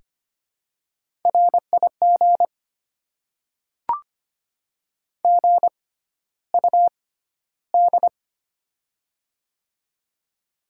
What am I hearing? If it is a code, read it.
Morse code, heard as a keyed tone, about 25 wpm, spaced as for 13 wpm, 700 Hz. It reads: RIG GUD
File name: Top 20 QSO Elements - 6x Character Spacing - 25wpm